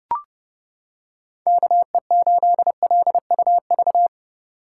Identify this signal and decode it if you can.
Morse code, heard as a keyed tone, about 29 wpm, 700 Hz. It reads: KE8LUV